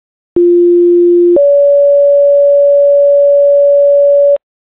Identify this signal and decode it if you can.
Two-tone page, heard as a one-second tone, then a three-second tone, about 350 then 570 Hz, with noise. It is heard from 0.4 to 4.4 s.